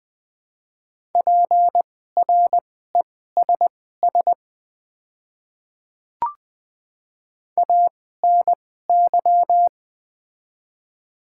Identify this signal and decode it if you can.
Morse code, heard as a keyed tone, about 20 wpm, spaced as for 14 wpm, 700 Hz. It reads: PRESS ANY
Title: 2x Character Spacing - Top 500 Words - 20wpm